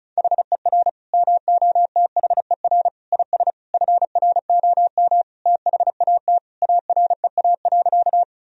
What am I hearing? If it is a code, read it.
Morse code, heard as a keyed tone, about 36 wpm, 700 Hz. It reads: HER MOTHER IS FROM THAT AREA.
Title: Sentences from Top 1000 Words - Review Part 2 - 35wpm